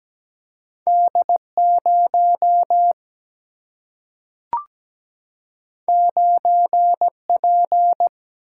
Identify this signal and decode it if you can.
Morse code, heard as a keyed tone, about 17 wpm, 700 Hz. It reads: D0 9P
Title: Single-Letter Single-Number 17wpm